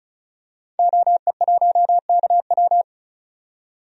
Morse code, heard as a keyed tone, about 36 wpm, 700 Hz. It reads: OE1KW
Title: DXCC-100 Call Signs - 5 Characters - 35wpm